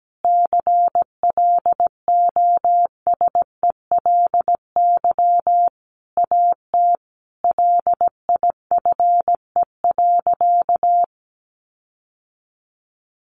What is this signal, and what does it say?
Morse code, heard as a keyed tone, about 17 wpm, 700 Hz. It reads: CLOSELY AT LIFE.
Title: Sentences from Top 1400 Words 17wpm